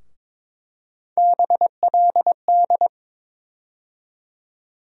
Morse code, heard as a keyed tone, about 21 wpm, 700 Hz. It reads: BLD
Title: USA Call Sign Postfixes - 3 Letters 22wpm